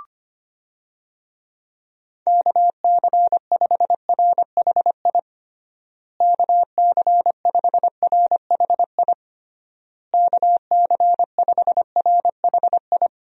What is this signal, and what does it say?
Morse code, heard as a keyed tone, about 25 wpm, 700 Hz. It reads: KC5RHI KC5RHI KC5RHI
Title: USA Call Signs - 6 Characters - Repeated 3x - 25wpm